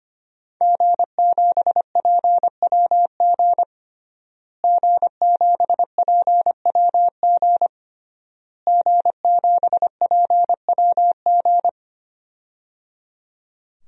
Morse code, heard as a keyed tone, about 25 wpm, 700 Hz. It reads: G7PWG G7PWG G7PWG
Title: DXCC-100 Call Signs - 5 Characters - Repeated 3x - 25wpm